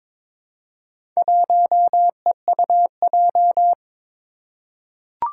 Morse code, heard as a keyed tone, about 22 wpm, 700 Hz. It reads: 1EUJ
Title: Four Challenging Letter-Number Combinations 22wpm